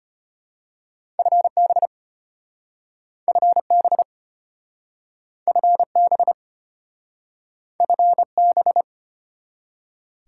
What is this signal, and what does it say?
Morse code, heard as a keyed tone, about 35 wpm, 690 Hz. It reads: FB FB FB FB